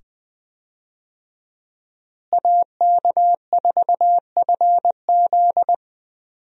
Morse code, heard as a keyed tone, about 20 wpm, 700 Hz. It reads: AK4FZ
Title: USA Call Signs - 5 Characters - Repeated 2x - 20wpm